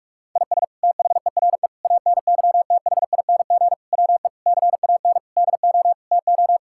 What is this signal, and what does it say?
Morse code, heard as a keyed tone, about 41 wpm, 700 Hz. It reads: IS THERE ANYTHING WE CAN DO TO
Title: Sentences from Top 1000 Words - Review Part 6 - 45wpm